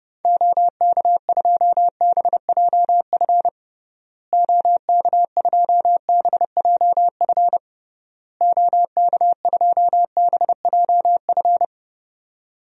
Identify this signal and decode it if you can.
Morse code, heard as a keyed tone, about 30 wpm, 700 Hz. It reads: OK2BJF OK2BJF OK2BJF
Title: DXCC-100 Call Signs - 6 Characters - Repeated 3x - 30wpm